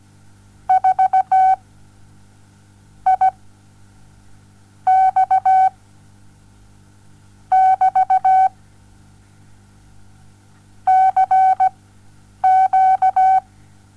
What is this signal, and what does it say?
Morse code, heard as a keyed tone, about 17 wpm, spaced as for 8 wpm, 760 Hz. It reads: 4 I X <BT> CQ
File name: Onlinelehrgang / morsen / DJ4UF Morsen07